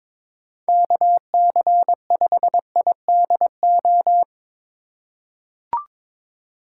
Morse code, heard as a keyed tone, about 21 wpm, 700 Hz. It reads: KC5IDO
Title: USA Call Signs - 2x Character Spacing - 22wpm